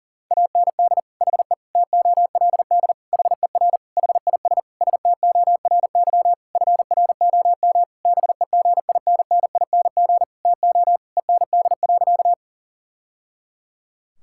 Morse code, heard as a keyed tone, about 40 wpm, 700 Hz. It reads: AND HE TOLD HER HIS STORY FROM BEGINNING TO END.
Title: Sentences from Top 1000 Words - Review Part 5 - 40wpm